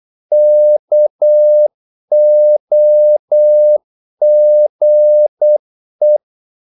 Morse code, heard as a keyed tone, about 8 wpm, 600 Hz. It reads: KOGE
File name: Korte ord DK 8wpm